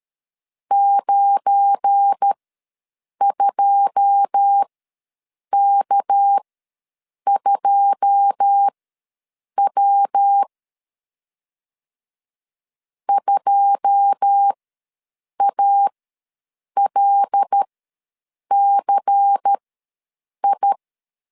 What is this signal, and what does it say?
Morse code, heard as a keyed tone, about 13 wpm, spaced as for 7 wpm, 800 Hz. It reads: 92K2W 2ALCI
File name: LEZIONI / 9 2 ws 12wpm 3406